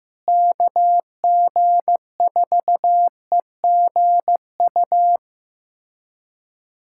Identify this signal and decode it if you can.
Morse code, heard as a keyed tone, about 15 wpm, 700 Hz. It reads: KG4EGU
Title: USA Callsign DE Callsign 15wpm